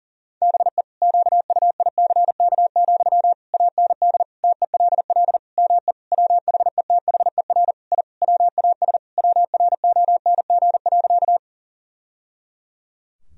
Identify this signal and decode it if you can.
Morse code, heard as a keyed tone, about 40 wpm, 700 Hz. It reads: BE QUICK, AND TELL ME WHETHER I WAS WRONG.
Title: Sentences from Top 1000 Words - Review Part 3 - 40wpm